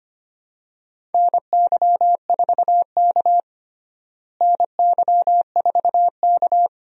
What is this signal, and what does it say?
Morse code, heard as a keyed tone, about 25 wpm, 700 Hz. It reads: NY4K NY4K
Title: USA Call Signs - 4 Characters - Repeated 3x - 25wpm